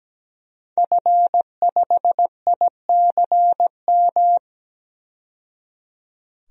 Morse code, heard as a keyed tone, about 17 wpm, 700 Hz. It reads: F5ICM
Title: DXCC-100 Call Signs - 5 Characters - 17wpm